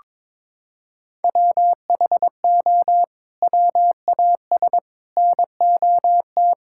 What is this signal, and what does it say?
Morse code, heard as a keyed tone, about 22 wpm, 700 Hz. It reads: WHO WAS NOT
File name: practice / 22wpm / Sets of 8 Words - 22wpm